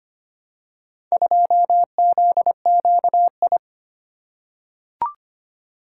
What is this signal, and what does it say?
Morse code, heard as a keyed tone, about 25 wpm, 700 Hz. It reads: 2ZQI
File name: practice / 25wpm / Four Letter-Number Combinations - 25wpm